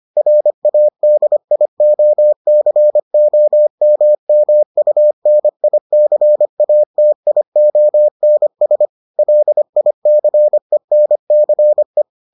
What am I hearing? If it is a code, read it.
Morse code, heard as a keyed tone, about 25 wpm, 600 Hz. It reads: RADIOCOMMUNICATIONS LICENCE